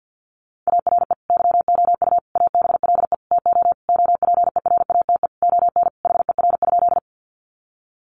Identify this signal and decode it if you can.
Morse code, heard as a keyed tone, about 50 wpm, 700 Hz. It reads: ARE YOU ABLE TO OPERATE ON HER?
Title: Sentences from Top 1500 Words 50wpm